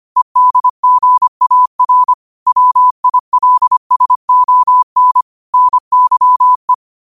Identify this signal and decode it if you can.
Morse code, heard as a keyed tone, about 25 wpm, 1000 Hz. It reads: EDGAR WILSON NYE
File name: qotdfiles / 25wpm